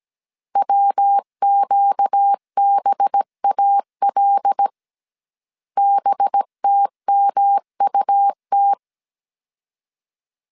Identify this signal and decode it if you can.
Morse code, heard as a keyed tone, about 17 wpm, 800 Hz. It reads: WQBAL BTMUT
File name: 16wpm